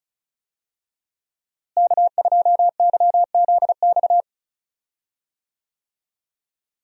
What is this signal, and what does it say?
Morse code, heard as a keyed tone, about 36 wpm, 700 Hz. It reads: K2YZX
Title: USA Call Signs - 5 Characters - Repeated 3x - 35wpm